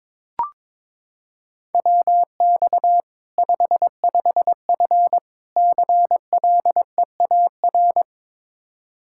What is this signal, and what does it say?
Morse code, heard as a keyed tone, about 22 wpm, 700 Hz. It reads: WX 55F CLEAR
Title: Practice QSO - Exchange 2 - Info - 22wpm